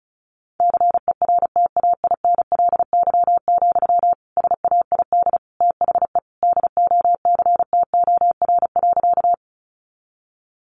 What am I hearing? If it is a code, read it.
Morse code, heard as a keyed tone, about 34 wpm, 700 Hz. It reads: CERTAINLY, SAID THE DOCTOR.
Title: Sentences from Top 1500 Words 35wpm